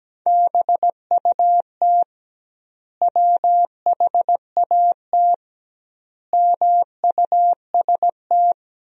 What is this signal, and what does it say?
Morse code, heard as a keyed tone, about 17 wpm, 700 Hz. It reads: BUT WHAT MUST
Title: 2x Word-Spacing - Sentences from Top 500 Words - 17wpm